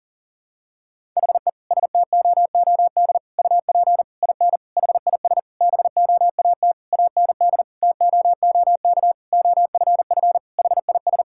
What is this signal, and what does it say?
Morse code, heard as a keyed tone, about 40 wpm, 700 Hz. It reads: HE STOOD UP IN HIS BOAT AND TOOK OFF HIS
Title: Sentences from Top 1600 Words 40wpm